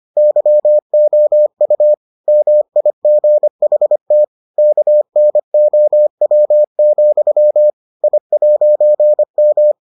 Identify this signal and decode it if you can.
Morse code, heard as a keyed tone, about 25 wpm, 600 Hz. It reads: YOU MIGHT KNOW, I'M